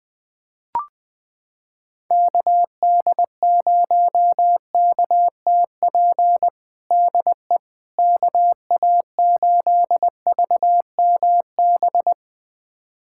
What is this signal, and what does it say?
Morse code, heard as a keyed tone, about 21 wpm, 700 Hz. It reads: KD0KTP DE KA8VMB
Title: USA Callsign DE Callsign 20wpm